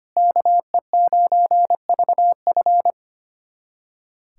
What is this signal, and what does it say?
Morse code, heard as a keyed tone, about 25 wpm, 700 Hz. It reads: KE9VF
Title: USA Callsign DE Callsign - 5 Characters - 25wpm